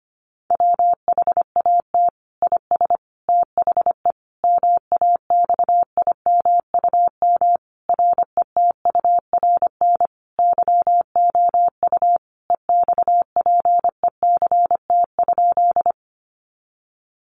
Morse code, heard as a keyed tone, about 25 wpm, 700 Hz. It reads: WHAT IS THE MAXIMUM RETURN YOU EXPECT?
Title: Sentences from Top 1300 Words 25wpm